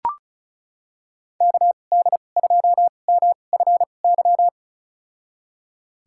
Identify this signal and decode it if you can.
Morse code, heard as a keyed tone, about 36 wpm, spaced as for 25 wpm, 700 Hz. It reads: KD2MFY
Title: USA Call Signs - 6 Characters - 2x Character Spacing - 35wpm